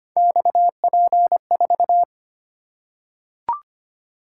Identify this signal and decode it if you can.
Morse code, heard as a keyed tone, about 25 wpm, 700 Hz. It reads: XP4
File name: Three Letter-Number Combinations - 25wpm